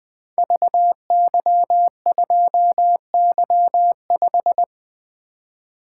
Morse code, heard as a keyed tone, about 20 wpm, 700 Hz. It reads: VY2Y5
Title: Code Groups - 5 Letter:Numbers - 20wpm